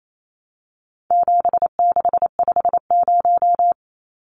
Morse code, heard as a keyed tone, about 28 wpm, 700 Hz. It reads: 7650
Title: Numbers - 4-Digit 28wpm